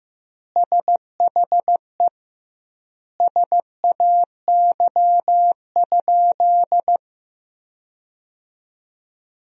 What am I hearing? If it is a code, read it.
Morse code, heard as a keyed tone, about 15 wpm, 700 Hz. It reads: SHE SAY?
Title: 2x Word-Spacing - Sentences from Top 500 Words - 15wpm